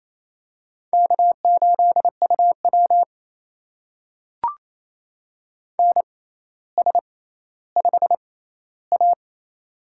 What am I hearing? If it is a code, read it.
Morse code, heard as a keyed tone, about 27 wpm, spaced as for 14 wpm, 700 Hz. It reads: K8UW NS5A
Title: USA Call Signs - 4 Characters - 6x Character Spacing - 28wpm